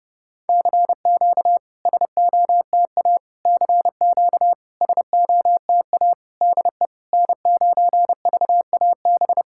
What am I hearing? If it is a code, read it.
Morse code, heard as a keyed tone, about 30 wpm, 690 Hz. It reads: CQ SOTA CQ SOTA DE N9VAB
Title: SOTA QSO Practice 30wpm